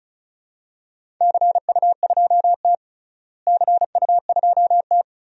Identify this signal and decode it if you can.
Morse code, heard as a keyed tone, about 34 wpm, 700 Hz. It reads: CU2T CU2T
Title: DXCC-100 Call Signs - 4 Characters - Repeated 3x - 35wpm